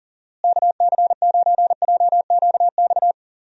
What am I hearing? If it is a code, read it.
Morse code, heard as a keyed tone, about 38 wpm, 700 Hz. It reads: KC9JQX
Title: USA Call Signs - 6 Characters - Repeated 3x - 40wpm